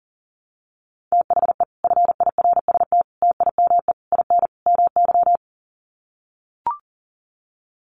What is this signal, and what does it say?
Morse code, heard as a keyed tone, about 40 wpm, 700 Hz. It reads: THE FIRST TIME IN MY
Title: Sets of 5 Words Encore 40wpm